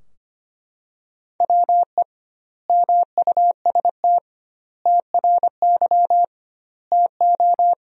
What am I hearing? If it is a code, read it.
Morse code, heard as a keyed tone, about 25 wpm, 700 Hz. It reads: WE MUST TRY TO
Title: 2x Word-Spacing - Sentences from Top 400 Words - 25wpm